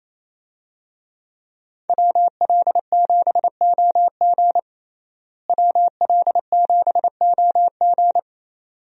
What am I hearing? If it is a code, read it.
Morse code, heard as a keyed tone, about 28 wpm, 700 Hz. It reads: WL7OG WL7OG
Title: USA Call Signs - 5 Characters - Repeated 2x - 28wpm